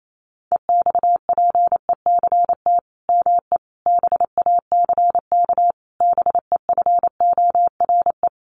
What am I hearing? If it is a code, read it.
Morse code, heard as a keyed tone, about 28 wpm, 700 Hz. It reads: EXPECT ME BACK BEFORE
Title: Sentences from Top 1100 Words 28wpm